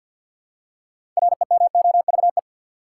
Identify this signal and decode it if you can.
Morse code, heard as a keyed tone, about 53 wpm, 700 Hz. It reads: REMOVE
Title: Six Letter Words - 50wpm